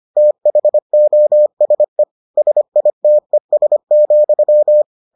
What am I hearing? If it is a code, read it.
Morse code, heard as a keyed tone, about 25 wpm, 600 Hz. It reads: THOSE SITES,